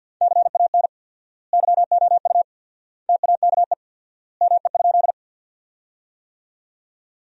Morse code, heard as a keyed tone, about 50 wpm, 700 Hz. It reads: CAN YOU TAKE ME?
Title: 4x Word-Spacing - Sentences from Top 200 Words - 50wpm